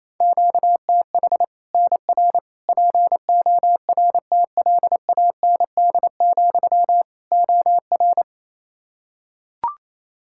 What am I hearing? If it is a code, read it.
Morse code, heard as a keyed tone, about 28 wpm, 700 Hz. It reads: QTH NR PORTLAND, OR
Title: Practice QSO - Exchange 1 - Beginning - 28wpm